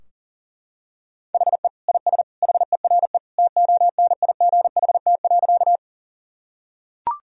Morse code, heard as a keyed tone, about 40 wpm, 700 Hz. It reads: HE IS HERE TONIGHT.
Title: Sentences from Top 1000 Words - Review Part 3 - 40wpm